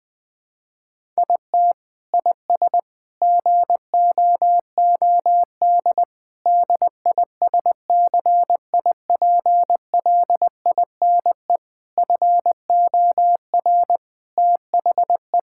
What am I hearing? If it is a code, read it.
Morse code, heard as a keyed tone, about 19 wpm, 700 Hz. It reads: IT IS GOOD DISCIPLINE FOR THE
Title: Sentences from Top 1000 Words - Review Part 6 - 20wpm